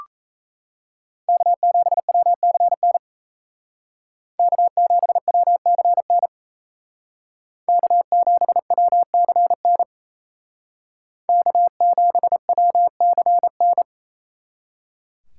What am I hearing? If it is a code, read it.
Morse code, heard as a keyed tone, about 36 wpm, 700 Hz. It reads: K7WCN K7WCN K7WCN K7WCN